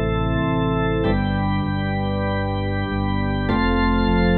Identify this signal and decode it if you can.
There is no code.